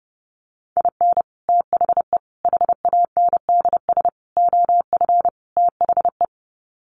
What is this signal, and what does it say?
Morse code, heard as a keyed tone, about 30 wpm, 700 Hz. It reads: IN THE HANDS OF THE
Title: Sets of 5 Words 30wpm